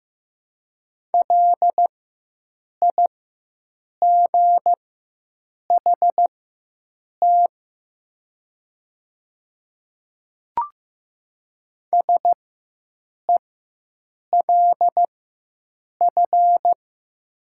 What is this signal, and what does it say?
Morse code, heard as a keyed tone, about 16 wpm, spaced as for 7 wpm, 700 Hz. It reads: LIGHT SELF
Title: 4x Character Spacing - Top 200 Words - 15wpm